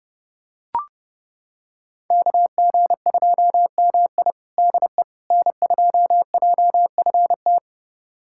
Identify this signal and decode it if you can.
Morse code, heard as a keyed tone, about 30 wpm, 700 Hz. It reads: KG2MI DE N2JFT
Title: USA Callsign DE Callsign 30wpm